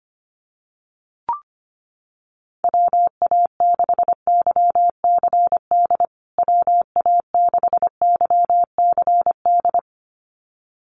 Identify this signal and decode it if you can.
Morse code, heard as a keyed tone, about 25 wpm, 700 Hz. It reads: WA6YCD WA6YCD
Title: USA Call Signs Repeated 2x 25wpm